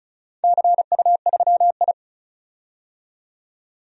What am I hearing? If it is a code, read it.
Morse code, heard as a keyed tone, about 34 wpm, 700 Hz. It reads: CU3I
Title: DXCC-100 Call Signs - 4 Characters - 35wpm